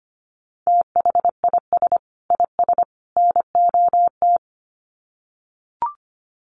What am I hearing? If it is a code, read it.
Morse code, heard as a keyed tone, about 25 wpm, 700 Hz. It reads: THIS IS NOT
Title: Sets of 3 Words 25wpm